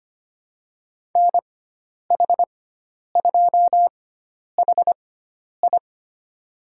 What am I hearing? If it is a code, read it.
Morse code, heard as a keyed tone, about 25 wpm, spaced as for 10 wpm, 700 Hz. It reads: NH2HI
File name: Lesson 16g - Letter c - Call Signs 25wpm